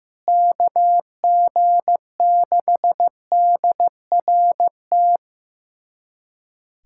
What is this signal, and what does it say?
Morse code, heard as a keyed tone, about 15 wpm, 700 Hz. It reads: KG6DRT
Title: USA Callsign DE Callsign - 6 Characters - 15wpm